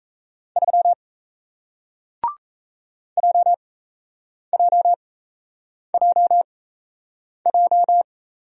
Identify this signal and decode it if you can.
Morse code, heard as a keyed tone, about 38 wpm, spaced as for 8 wpm, 700 Hz. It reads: 3 JJJJ